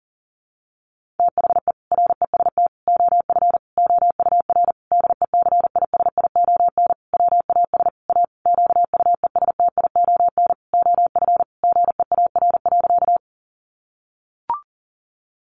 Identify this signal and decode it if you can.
Morse code, heard as a keyed tone, about 40 wpm, 700 Hz. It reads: THE REST OF OUR DECISION WAS A QUESTION OF GEAR.
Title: Sentences from Top 1400 Words 40wpm